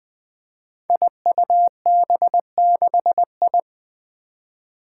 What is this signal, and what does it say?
Morse code, heard as a keyed tone, about 21 wpm, 700 Hz. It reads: IUB6I
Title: Five Challenging Letter-Number Combinations 20wpm